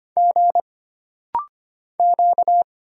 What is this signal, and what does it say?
Morse code, heard as a keyed tone, about 25 wpm, 700 Hz. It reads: G Q
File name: Single Character (Letter:Number:Punctuation:Prosign) - Rapid-Fire - 25wpm